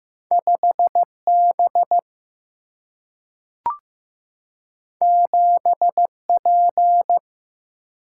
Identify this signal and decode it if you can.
Morse code, heard as a keyed tone, about 15 wpm, 700 Hz. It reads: 5B 7P
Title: Single-Letter Single-Number 15wpm